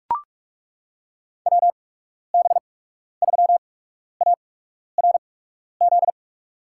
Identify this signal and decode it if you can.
Morse code, heard as a keyed tone, about 42 wpm, spaced as for 12 wpm, 700 Hz. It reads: WB3ARZ
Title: USA Call Signs - 6 Characters - 8x Character Spacing - 45wpm